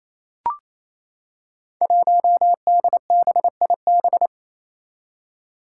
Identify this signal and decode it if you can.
Morse code, heard as a keyed tone, about 29 wpm, 700 Hz. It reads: 1DBIB
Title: Five Challenging Letter-Number Combinations 28wpm